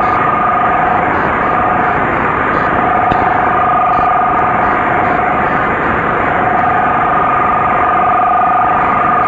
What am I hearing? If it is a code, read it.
Two-tone teleprinter signal, about 500 Hz shift, with speech.